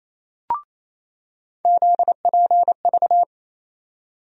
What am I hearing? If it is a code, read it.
Morse code, heard as a keyed tone, about 28 wpm, 700 Hz. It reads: ZPV
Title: USA Call Sign Postfixes - 3 Letters 28wpm